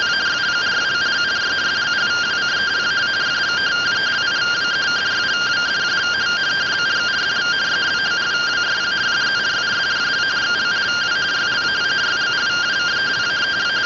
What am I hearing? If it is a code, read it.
Two-tone teleprinter signal, about 170 Hz shift.